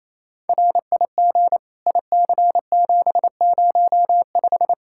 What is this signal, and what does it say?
Morse code, heard as a keyed tone, about 28 wpm, 700 Hz. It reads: RIG IC705
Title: Practice QSO - Exchange 2 - Info - 28wpm